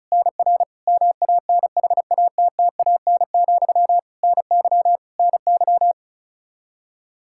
Morse code, heard as a keyed tone, about 36 wpm, 700 Hz. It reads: NR MANHATTAN, NY NY
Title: Practice QSO - Exchange 1 - Beginning - 35wpm